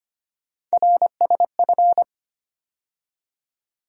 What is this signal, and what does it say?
Morse code, heard as a keyed tone, about 25 wpm, 700 Hz. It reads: RSF
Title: USA Call Sign Postfixes - 3 Letters 25wpm